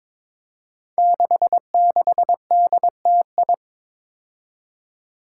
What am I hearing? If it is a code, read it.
Morse code, heard as a keyed tone, about 22 wpm, 700 Hz. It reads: 66DTI
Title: Five Challenging Letter-Number Combinations 22wpm